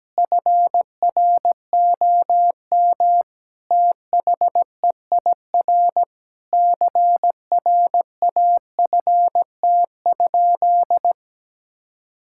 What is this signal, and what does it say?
Morse code, heard as a keyed tone, about 17 wpm, 700 Hz. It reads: FROM THEIR CRAFT?
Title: Sentences from Top 1000 Words - Review Part 6 - 17wpm